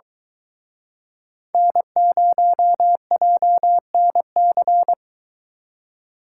Morse code, heard as a keyed tone, about 22 wpm, 700 Hz. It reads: N0JNC